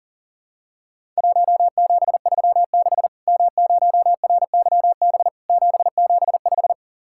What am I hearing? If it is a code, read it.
Morse code, heard as a keyed tone, about 40 wpm, 700 Hz. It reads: 1736 M0RYB 775
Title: Worked All Europe DX Contest - QTC Exchange - 40wpm